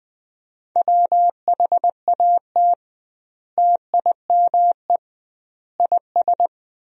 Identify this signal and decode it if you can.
Morse code, heard as a keyed tone, about 21 wpm, 700 Hz. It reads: WHAT TIME IS
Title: 2x Word-Spacing - Sentences from Top 100 Words - 20wpm